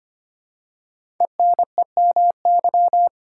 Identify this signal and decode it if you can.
Morse code, heard as a keyed tone, about 25 wpm, 700 Hz. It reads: ENEMY
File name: 5 Letter Words - First 3 Letters Given - 25wpm